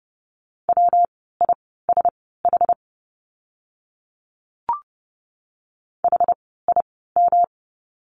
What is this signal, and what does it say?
Morse code, heard as a keyed tone, about 30 wpm, spaced as for 17 wpm, 700 Hz. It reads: WISH HIM
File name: course / 30:10wpm / 115 Lesson 19f - Letter w - Words 30wpm